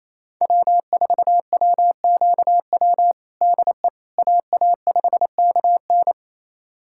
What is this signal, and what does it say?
Morse code, heard as a keyed tone, about 27 wpm, 700 Hz. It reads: W4WQW DE AA5KN
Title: USA Callsign DE Callsign - 5 Characters - 28wpm